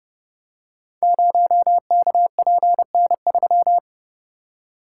Morse code, heard as a keyed tone, about 30 wpm, 700 Hz. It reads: 0KPN3
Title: Code Groups - 5 Letter:Numbers - 30wpm